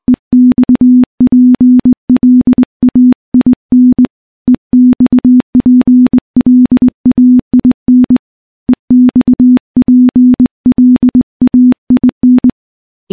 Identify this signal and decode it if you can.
Morse code, heard as a keyed tone, about 20 wpm, 250 Hz. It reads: EXPLAIN EXPLAIN EXPLAIN
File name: morse / morse-training-20WPM